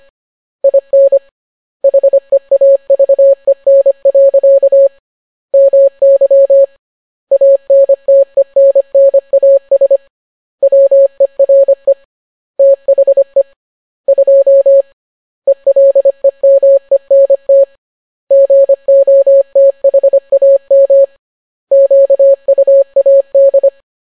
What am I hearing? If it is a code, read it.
Morse code, heard as a keyed tone, about 25 wpm, 550 Hz. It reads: IN HEAVEN. MY ANTENNAS WERE THE 2 ELEMENT GOTHAM QUAD